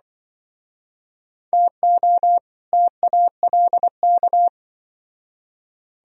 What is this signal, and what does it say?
Morse code, heard as a keyed tone, about 24 wpm, 700 Hz. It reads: TO TALK
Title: Speed-Racing - Sets of 2 Words 20wpm